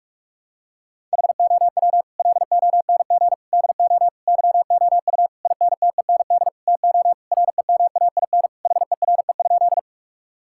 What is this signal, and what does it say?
Morse code, heard as a keyed tone, about 41 wpm, 700 Hz. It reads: HOW LONG DO YOU INTEND TO REMAIN HERE?
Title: Sentences from Top 1600 Words 45wpm